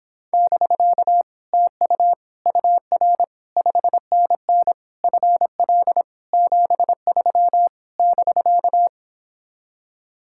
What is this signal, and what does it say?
Morse code, heard as a keyed tone, about 29 wpm, 700 Hz. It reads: <BK> TU UR 5NN FL 73 <BK>